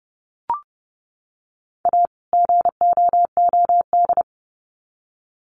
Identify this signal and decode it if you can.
Morse code, heard as a keyed tone, about 30 wpm, 700 Hz. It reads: A GOOD
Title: Sets of 2 Words 30wpm